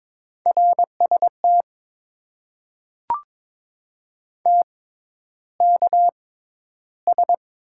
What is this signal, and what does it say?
Morse code, heard as a keyed tone, about 22 wpm, spaced as for 11 wpm, 700 Hz. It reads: RST TKS
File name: Top 90 QSO Elements - 6x Character Spacing - 22wpm